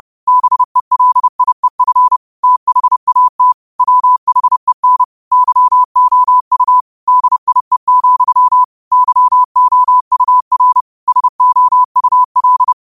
Morse code, heard as a keyed tone, about 30 wpm, 1000 Hz. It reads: BELIEF THAT WHEN YOU DIE, YOUR SOUL
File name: qotdfiles / 30wpm